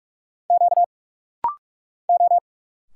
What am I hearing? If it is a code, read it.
Morse code, heard as a keyed tone, about 46 wpm, 700 Hz. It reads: Q O
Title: Single Letters - Rapid-Fire 45wpm